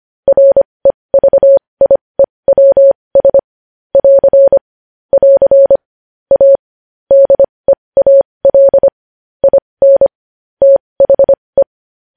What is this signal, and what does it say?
Morse code, heard as a keyed tone, about 25 wpm, spaced as for 20 wpm, 550 Hz. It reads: REVIEWS <AR> <AR> A DEAL IN THE
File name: podcast / News headlines-1x-25x20